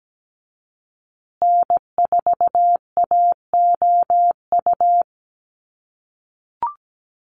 Morse code, heard as a keyed tone, about 17 wpm, 700 Hz. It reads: N4AOU
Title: USA Call Signs Repeated 2x 17wpm